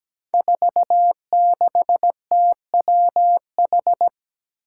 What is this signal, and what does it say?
Morse code, heard as a keyed tone, about 17 wpm, 700 Hz. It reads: 46TWH
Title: Five Challenging Letter-Number Combinations 17wpm